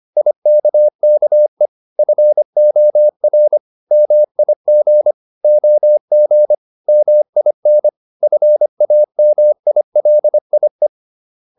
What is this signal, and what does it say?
Morse code, heard as a keyed tone, about 25 wpm, 600 Hz. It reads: IKKE FOR MIG OG MIN FAMILIE
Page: Ferie 25 wpm